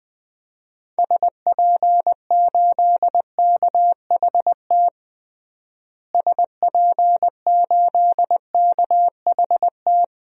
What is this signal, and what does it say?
Morse code, heard as a keyed tone, about 20 wpm, 700 Hz. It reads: SP8KHT SP8KHT
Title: DXCC-100 Call Signs - 6 Characters - Repeated 2x - 20wpm